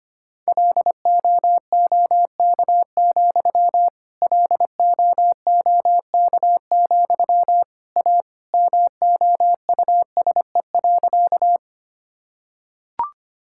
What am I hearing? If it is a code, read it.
Morse code, heard as a keyed tone, about 25 wpm, 700 Hz. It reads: LOOK, LOOK, A MOUSE.
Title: Sentences from Top 1000 Words - Review Part 5 - 25wpm